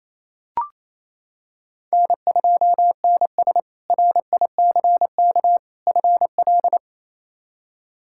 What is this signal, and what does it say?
Morse code, heard as a keyed tone, about 28 wpm, 700 Hz. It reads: N2NS RICK FL
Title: K1USN SST QSO Practice - 28wpm